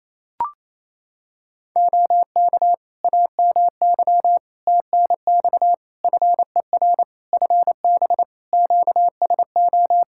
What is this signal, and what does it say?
Morse code, heard as a keyed tone, about 27 wpm, 700 Hz. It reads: OK AMY TNX FER FB QSO
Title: Practice QSO - Exchange 4 - Ending - 28wpm